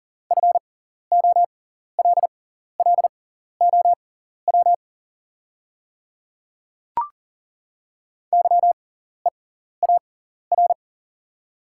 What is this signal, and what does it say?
Morse code, heard as a keyed tone, about 39 wpm, spaced as for 14 wpm, 700 Hz. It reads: FOLLOW YEAR
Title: 6x Character Spacing - Top 200 Words - 40wpm